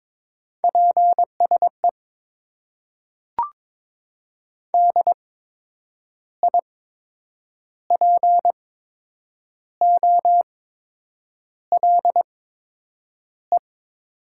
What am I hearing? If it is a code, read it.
Morse code, heard as a keyed tone, about 22 wpm, spaced as for 8 wpm, 700 Hz. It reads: PSE DIPOLE
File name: Top 90 QSO Elements - 8x Character Spacing - 22wpm